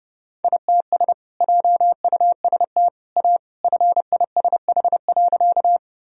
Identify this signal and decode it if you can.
Morse code, heard as a keyed tone, about 30 wpm, 700 Hz. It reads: ITS JUST A FISH.
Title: Sentences from Top 1000 Words - Review Part 1 - 30wpm